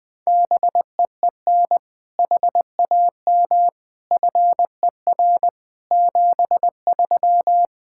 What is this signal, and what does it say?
Morse code, heard as a keyed tone, about 20 wpm, 700 Hz. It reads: BEEN HAM FER 73
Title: Practice QSO - Exchange 3 - More Info - 20wpm